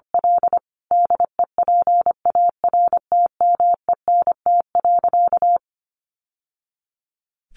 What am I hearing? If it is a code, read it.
Morse code, heard as a keyed tone, about 25 wpm, 700 Hz. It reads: L DEPARTMENT.
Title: Sentences from Top 1200 Words 25wpm